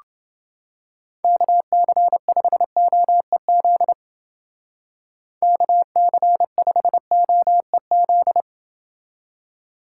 Morse code, heard as a keyed tone, about 28 wpm, 700 Hz. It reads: KC5OEZ KC5OEZ